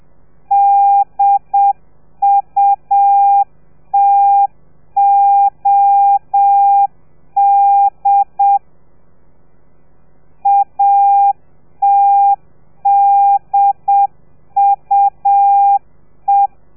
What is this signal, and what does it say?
Morse code, heard as a keyed tone, about 7 wpm, 800 Hz. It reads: DUTOD ATDUE